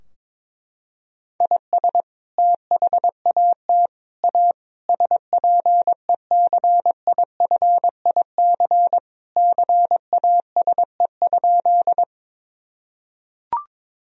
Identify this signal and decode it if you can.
Morse code, heard as a keyed tone, about 22 wpm, 700 Hz. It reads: IS THAT A SPECIFIC CASE?
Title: Sentences from Top 1000 Words - Review Part 3 - 22wpm